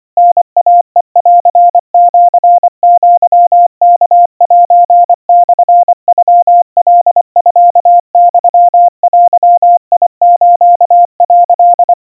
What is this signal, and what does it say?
Morse code (Japanese Wabun code), heard as a keyed tone, about 24 wpm, 700 Hz. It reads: タイヘンシアワセモノカミユデス」
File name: 90 cpm 700 Hz Crispy